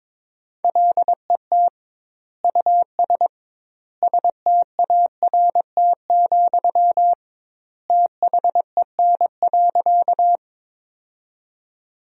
Morse code, heard as a keyed tone, about 22 wpm, 700 Hz. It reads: LET US START, THEN.
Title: 2x Word-Spacing - Sentences from Top 300 Words - 22wpm